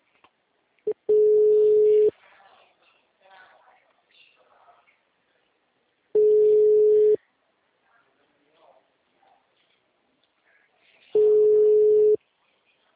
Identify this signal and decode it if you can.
Ringback tone (Europe, 425 Hz), with speech.